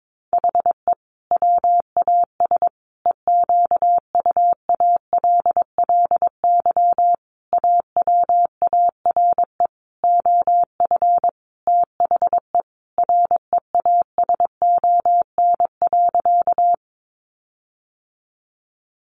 Morse code, heard as a keyed tone, about 22 wpm, 700 Hz. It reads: HE WAS EQUALLY AWARE OF THE REASON.
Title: Sentences from Top 1400 Words 22wpm